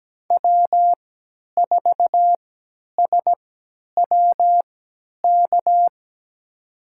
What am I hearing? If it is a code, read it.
Morse code, heard as a keyed tone, about 17 wpm, spaced as for 10 wpm, 700 Hz. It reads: W4SWK
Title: USA Call Signs - 5 Characters - 3x Character Spacing - 17wpm